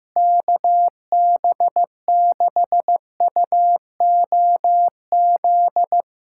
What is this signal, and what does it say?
Morse code, heard as a keyed tone, about 15 wpm, 700 Hz. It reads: KB6UOZ
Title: USA Callsign DE Callsign 15wpm